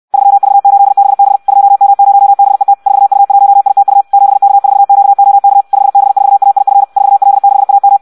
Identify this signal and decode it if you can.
Morse code, heard as a keyed tone, about 64 wpm, 800 Hz. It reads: 3V8BB 2M0BET 4U1ITU 7K4QOK 5B4AIF 4L6AM
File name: Audio practica CW / qrq-tx-66 2